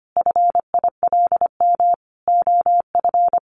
Morse code, heard as a keyed tone, about 25 wpm, 700 Hz. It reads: FILM OF
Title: A Sound Of Thunder 25wpm